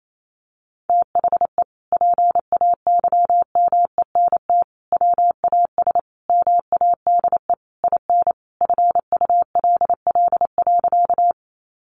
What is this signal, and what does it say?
Morse code, heard as a keyed tone, about 28 wpm, 700 Hz. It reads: THE PAYMENT WAS MADE IN FULL.
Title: Sentences from Top 1200 Words 28wpm